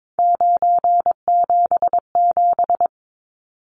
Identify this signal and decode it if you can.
Morse code, heard as a keyed tone, about 22 wpm, 700 Hz. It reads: 977
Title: Numbers - 3-Digit 22wpm